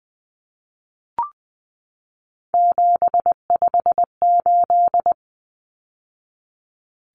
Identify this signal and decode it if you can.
Morse code, heard as a keyed tone, about 20 wpm, 700 Hz. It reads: 758
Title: Numbers - 3-Digit 20wpm